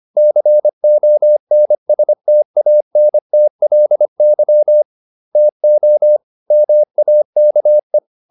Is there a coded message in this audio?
Yes: Morse code from 0.2 to 8.0 s, heard as a keyed tone, about 25 wpm, 600 Hz. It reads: CONSTANTLY TO MAKE